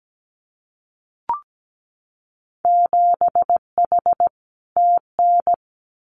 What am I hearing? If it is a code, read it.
Morse code, heard as a keyed tone, about 17 wpm, 700 Hz. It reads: 7H TN
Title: Winter Field Day Exchanges Encore 17wpm